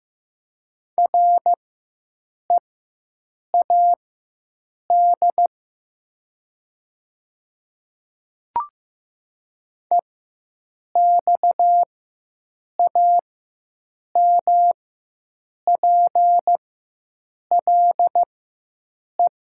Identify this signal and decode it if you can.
Morse code, heard as a keyed tone, about 15 wpm, spaced as for 7 wpm, 700 Hz. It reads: READ EXAMPLE
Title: 4x Character Spacing - Top 300 Words - 15wpm